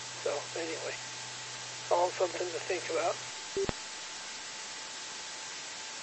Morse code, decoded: none

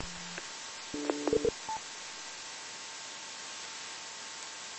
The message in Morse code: none

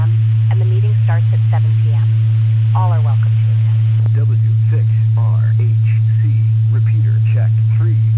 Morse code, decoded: none

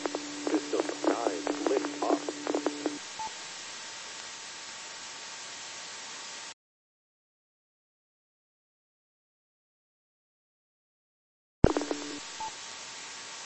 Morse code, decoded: none

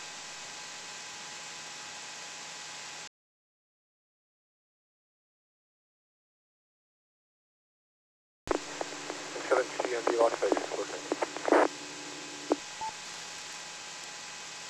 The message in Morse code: none

